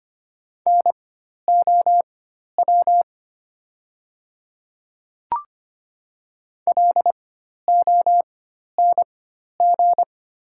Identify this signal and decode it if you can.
Morse code, heard as a keyed tone, about 25 wpm, spaced as for 12 wpm, 700 Hz. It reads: NOW LONG
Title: 4x Character Spacing - Top 100 Words - 25wpm